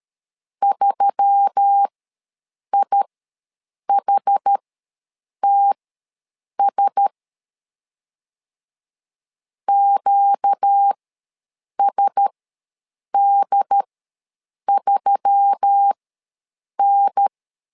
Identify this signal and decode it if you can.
Morse code, heard as a keyed tone, about 13 wpm, spaced as for 7 wpm, 800 Hz. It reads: 3IHTS QSD3N
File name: LEZIONI / 4 Q1S3 ws 12wpm 2038